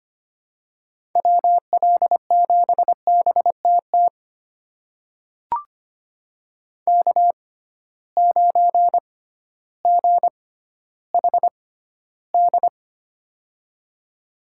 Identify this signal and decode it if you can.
Morse code, heard as a keyed tone, about 25 wpm, spaced as for 13 wpm, 700 Hz. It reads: WL7BTT K9GHD
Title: USA Call Signs - 6x Character Spacing - 25wpm